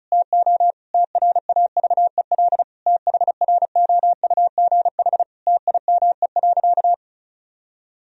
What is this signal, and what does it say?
Morse code, heard as a keyed tone, about 34 wpm, 700 Hz. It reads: TO TRAVEL THROUGH TIME.
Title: Sentences from Top 1000 Words - Review Part 2 - 35wpm